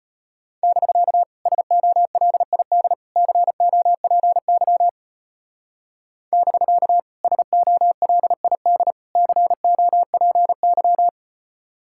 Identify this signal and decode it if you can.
Morse code, heard as a keyed tone, about 30 wpm, 700 Hz. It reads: <BK> SOLID COPY <BK> SOLID COPY
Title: Speed-Racing Common QSO Phrases - 25wpm